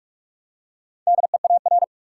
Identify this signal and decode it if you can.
Morse code, heard as a keyed tone, about 42 wpm, 700 Hz. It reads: DEAR
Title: Four Letter Words - 45wpm